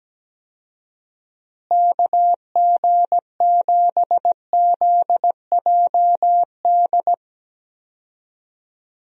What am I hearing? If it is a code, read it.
Morse code, heard as a keyed tone, about 17 wpm, 700 Hz. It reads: KG7ZJD
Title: USA Call Signs - 6 Characters - Repeated 3x - 17wpm